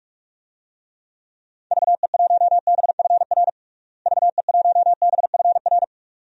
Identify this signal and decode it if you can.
Morse code, heard as a keyed tone, about 46 wpm, 700 Hz. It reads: VE1BFR VE1BFR
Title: DXCC-100 Call Signs - 6 Characters - Repeated 2x - 45wpm